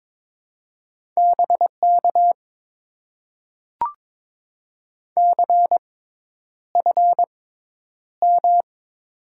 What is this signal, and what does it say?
Morse code, heard as a keyed tone, about 22 wpm, spaced as for 10 wpm, 700 Hz. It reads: BK CFM
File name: Top 90 QSO Elements - 6x Character Spacing - 22wpm